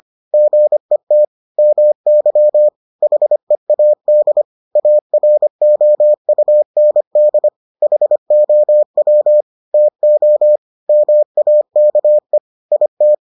Morse code, heard as a keyed tone, about 25 wpm, 600 Hz. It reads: GET MY HEAD AROUND HOW TO MAKE IT